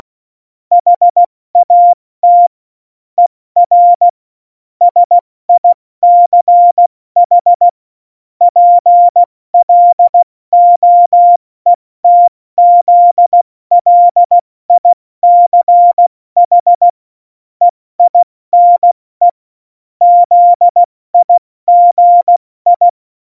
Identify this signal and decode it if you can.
Morse code, heard as a keyed tone, about 16 wpm, 700 Hz. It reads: HAT ER SICH PLOETZLICH EINE ZIGI